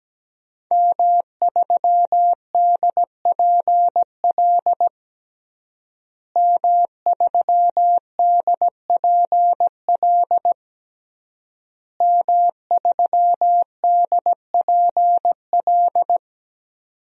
Morse code, heard as a keyed tone, about 17 wpm, 700 Hz. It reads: M3DPL M3DPL M3DPL
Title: DXCC-100 Call Signs - 5 Characters - Repeated 3x - 17wpm